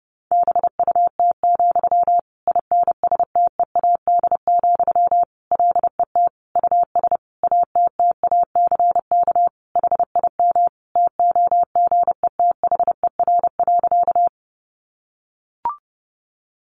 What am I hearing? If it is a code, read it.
Morse code, heard as a keyed tone, about 30 wpm, 700 Hz. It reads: BUT, INSTEAD, LET US ATTACK HIM TOGETHER.
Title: Sentences from Top 1500 Words 30wpm